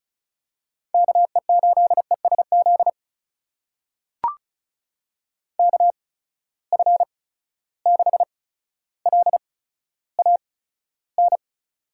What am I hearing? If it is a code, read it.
Morse code, heard as a keyed tone, about 36 wpm, spaced as for 15 wpm, 700 Hz. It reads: KE8ESZ KF6LAN
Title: USA Call Signs - 6 Characters - 8x Character Spacing - 35wpm